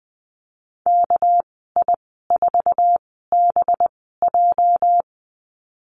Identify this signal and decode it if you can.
Morse code, heard as a keyed tone, about 20 wpm, spaced as for 15 wpm, 700 Hz. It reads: KI4BJ
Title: 231 Lesson 37g - Letter x - Call Signs 20wpm